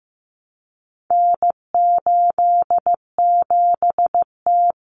Morse code, heard as a keyed tone, about 15 wpm, 700 Hz. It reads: N87T
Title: Common Cut Numbers - 4-Digit 15wpm